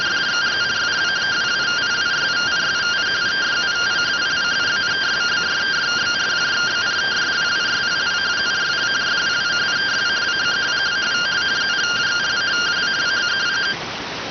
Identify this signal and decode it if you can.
Two-tone teleprinter signal, about 170 Hz shift.